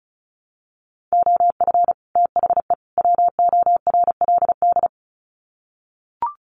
Morse code, heard as a keyed tone, about 34 wpm, 700 Hz. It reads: OF THE WORLD
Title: Sets of 3 Words 35wpm